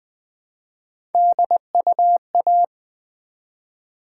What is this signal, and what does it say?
Morse code, heard as a keyed tone, about 20 wpm, 700 Hz. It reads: DUA
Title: Build A Word - 1 Letter At A Time - 3 to 5-Letter Words - 20wpm